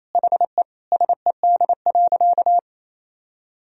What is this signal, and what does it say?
Morse code, heard as a keyed tone, about 28 wpm, 700 Hz. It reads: HE SED.
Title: Sentences from Top 100 Words and Abbreviations 28wpm